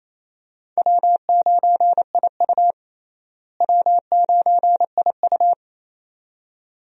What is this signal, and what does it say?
Morse code, heard as a keyed tone, about 28 wpm, 700 Hz. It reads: W9IU W9IU
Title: USA Call Signs - 4 Characters - Repeated 2x - 28wpm